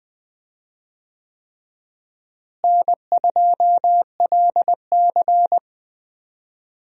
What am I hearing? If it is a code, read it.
Morse code, heard as a keyed tone, about 20 wpm, 700 Hz. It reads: N2LC
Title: USA Call Signs - 4 Characters - Repeated 3x - 20wpm